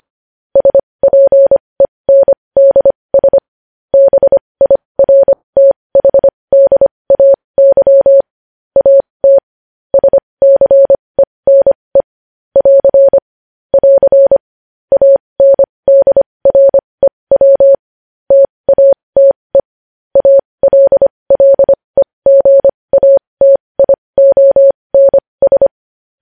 Morse code, heard as a keyed tone, about 25 wpm, spaced as for 20 wpm, 550 Hz. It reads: SPENDS BIRTHDAY AT SCENE <AR> <AR> ANDREW TATE ALLEGATIONS
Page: news headlines in morse code at 20 wpm